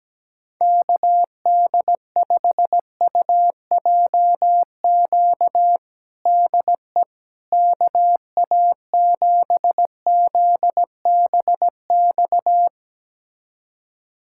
Morse code, heard as a keyed tone, about 17 wpm, 700 Hz. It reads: KD5UJQ DE KA7ZBX
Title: USA Callsign DE Callsign - 6 Characters - 17wpm